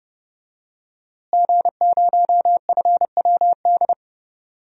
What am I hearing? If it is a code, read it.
Morse code, heard as a keyed tone, about 30 wpm, 700 Hz. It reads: G0FWD
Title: DXCC-100 Call Signs - 5 Characters - Repeated 2x - 30wpm